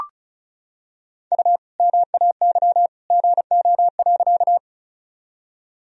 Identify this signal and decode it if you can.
Morse code, heard as a keyed tone, about 34 wpm, 700 Hz. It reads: U MAY GO.
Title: Sentences from Top 100 Words and Abbreviations 35wpm